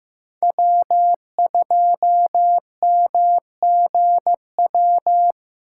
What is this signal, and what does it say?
Morse code, heard as a keyed tone, about 18 wpm, 700 Hz. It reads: W2MGW
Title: Speed Racing - USA 1x3 Call Signs 15wpm